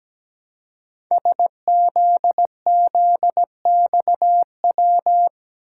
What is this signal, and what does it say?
Morse code, heard as a keyed tone, about 17 wpm, 700 Hz. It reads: SZZXW
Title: Code Groups - 5 Letters - 17wpm